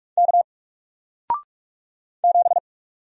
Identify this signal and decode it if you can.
Morse code, heard as a keyed tone, about 44 wpm, 700 Hz. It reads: K 7
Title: Single Letter-Number Alt 45wpm